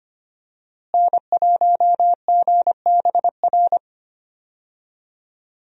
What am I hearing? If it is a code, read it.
Morse code, heard as a keyed tone, about 25 wpm, 700 Hz. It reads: N1GBR
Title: USA Call Signs - 5 Characters - Repeated 3x - 25wpm